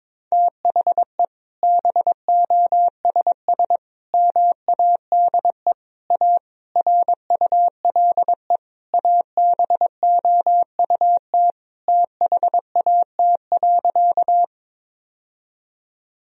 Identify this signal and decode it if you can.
Morse code, heard as a keyed tone, about 22 wpm, 700 Hz. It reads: THE BOSS MADE A RULE ABOUT THAT.
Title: Sentences from Top 1000 Words - Review Part 4 - 22wpm